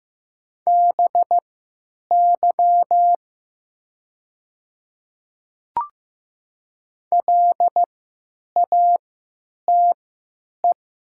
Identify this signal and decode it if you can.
Morse code, heard as a keyed tone, about 15 wpm, spaced as for 8 wpm, 700 Hz. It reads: BY LATE
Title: 3x Character Spacing - Top 300 Words - 15wpm